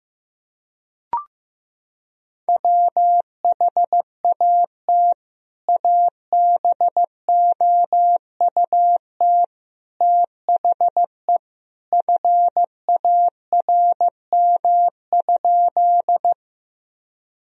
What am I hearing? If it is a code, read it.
Morse code, heard as a keyed tone, about 15 wpm, 700 Hz. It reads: WHAT ABOUT THE FARM?
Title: Sentences from Top 1000 Words - Review Part 2 - 15wpm